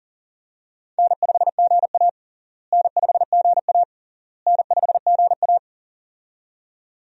Morse code, heard as a keyed tone, about 40 wpm, 700 Hz. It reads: N5GA N5GA N5GA
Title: USA Call Signs - 4 Characters - Repeated 3x - 40wpm